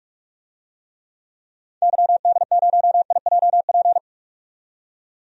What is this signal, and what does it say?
Morse code, heard as a keyed tone, about 46 wpm, 700 Hz. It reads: YD0IJP
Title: DXCC-100 Call Signs - 6 Characters - 45wpm